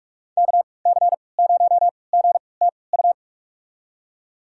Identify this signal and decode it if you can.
Morse code, heard as a keyed tone, about 46 wpm, spaced as for 26 wpm, 700 Hz. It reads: KC0GTU
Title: USA Call Signs - 6 Characters - 3x Character Spacing - 45wpm